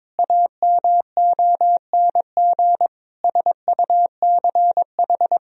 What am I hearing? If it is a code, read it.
Morse code, heard as a keyed tone, about 22 wpm, 700 Hz. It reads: AMONG SUCH
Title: Sentences from Top 1000 Words - Review Part 3 - 22wpm